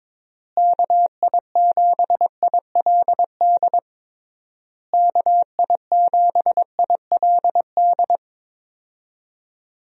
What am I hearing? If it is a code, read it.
Morse code, heard as a keyed tone, about 22 wpm, 700 Hz. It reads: KI7ILD KI7ILD
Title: USA Call Signs - 6 Characters - Repeated 3x - 22wpm